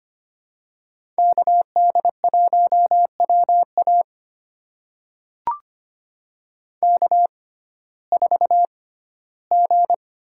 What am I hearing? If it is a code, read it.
Morse code, heard as a keyed tone, about 25 wpm, spaced as for 15 wpm, 700 Hz. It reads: KD1WA K4G
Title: USA Call Signs - 5 Characters - 6x Character Spacing - 25wpm